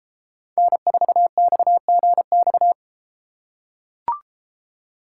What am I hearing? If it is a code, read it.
Morse code, heard as a keyed tone, about 32 wpm, 700 Hz. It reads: N4XGX